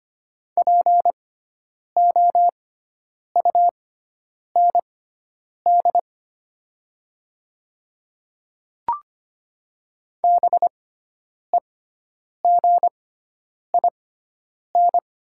Morse code, heard as a keyed tone, about 25 wpm, spaced as for 9 wpm, 700 Hz. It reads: POUND BEGIN
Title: 6x Character Spacing - Top 500 Words - 25wpm